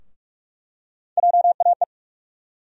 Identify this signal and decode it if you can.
Morse code, heard as a keyed tone, about 47 wpm, 700 Hz. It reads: JAE
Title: USA Call Sign Postfixes - 3 Letters 45wpm